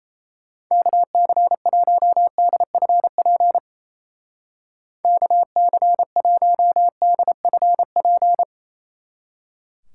Morse code, heard as a keyed tone, about 37 wpm, 700 Hz. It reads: KC1DFP KC1DFP